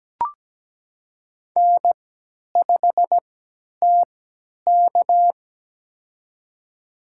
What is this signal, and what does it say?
Morse code, heard as a keyed tone, about 17 wpm, spaced as for 10 wpm, 700 Hz. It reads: N5TK
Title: USA Call Signs - 4 Characters - 3x Character Spacing - 17wpm